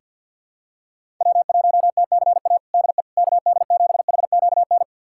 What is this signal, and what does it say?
Morse code, heard as a keyed tone, about 50 wpm, 700 Hz. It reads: W1TCA DE KD7HQN
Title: USA Callsign DE Callsign 50wpm